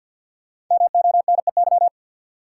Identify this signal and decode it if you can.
Morse code, heard as a keyed tone, about 47 wpm, 700 Hz. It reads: MONEY
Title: Sets of 4 Words - New Word Drill - 50wpm